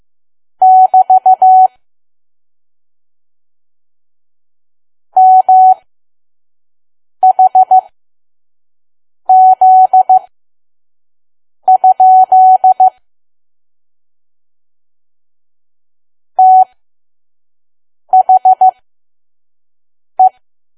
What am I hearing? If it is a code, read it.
Morse code, heard as a keyed tone, about 15 wpm, spaced as for 5 wpm, 750 Hz. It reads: <BT> MHZ? THE
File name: Archive / 5 WPM files / PM